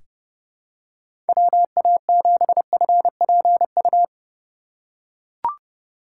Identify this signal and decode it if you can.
Morse code, heard as a keyed tone, about 30 wpm, 700 Hz. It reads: WA7FPU
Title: USA Call Signs - 6 Characters - 4x Character Spacing - 30wpm